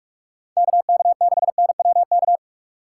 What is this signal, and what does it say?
Morse code, heard as a keyed tone, about 46 wpm, 700 Hz. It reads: KK6NWK